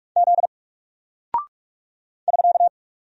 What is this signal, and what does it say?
Morse code, heard as a keyed tone, about 46 wpm, 700 Hz. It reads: Z <SK>
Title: Single Character (Letter:Number:Punctuation:Prosign) 45wpm